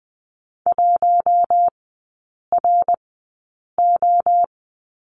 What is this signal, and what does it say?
Morse code, heard as a keyed tone, about 19 wpm, spaced as for 8 wpm, 700 Hz. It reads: 1RO
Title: 074 Lesson 13e - Letter L - Three Characters 20wpm